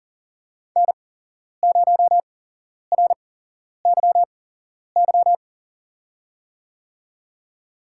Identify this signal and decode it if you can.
Morse code, heard as a keyed tone, about 40 wpm, spaced as for 11 wpm, 700 Hz. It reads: N0RYY
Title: USA Call Signs - 8x Character Spacing - 40wpm